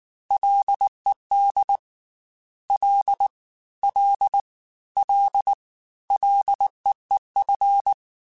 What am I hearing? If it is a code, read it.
Morse code, heard as a keyed tone, about 19 wpm, 800 Hz. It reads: LED L L L LEEF
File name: Ls / Single L m3s / Single L Prefixe 3x2 Twice @18